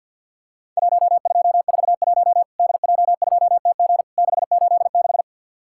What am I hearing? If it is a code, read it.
Morse code, heard as a keyed tone, about 50 wpm, 700 Hz. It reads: 1241 DJ2TG 686
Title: Worked All Europe DX Contest - QTC Exchange with Repeats - 50wpm